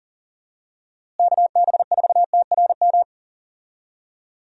Morse code, heard as a keyed tone, about 37 wpm, 700 Hz. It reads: KB4TRM
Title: USA Call Signs - 6 Characters - Repeated 2x - 40wpm